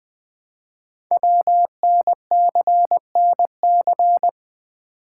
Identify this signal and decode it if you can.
Morse code, heard as a keyed tone, about 20 wpm, 700 Hz. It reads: WNCNC
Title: Code Groups - 5 Letters - 20wpm